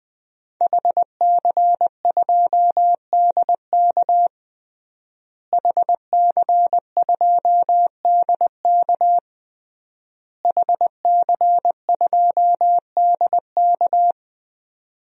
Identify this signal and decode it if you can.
Morse code, heard as a keyed tone, about 20 wpm, 700 Hz. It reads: HC2DK HC2DK HC2DK
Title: DXCC-100 Call Signs - 5 Characters - Repeated 3x - 20wpm